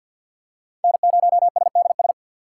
Morse code, heard as a keyed tone, about 48 wpm, 700 Hz. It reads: N0SDS
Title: USA Call Signs - 3x Character Spacing - 50wpm